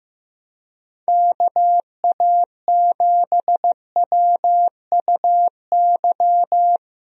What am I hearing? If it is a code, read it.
Morse code, heard as a keyed tone, about 15 wpm, 700 Hz. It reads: KA7WUY
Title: USA Call Signs - 6 Characters - Repeated 2x - 15wpm